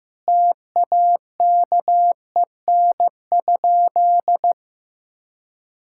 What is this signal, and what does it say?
Morse code, heard as a keyed tone, about 16 wpm, 700 Hz. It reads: TAKEN?
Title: Sentences from Top 1000 Words - Review Part 6 - 15wpm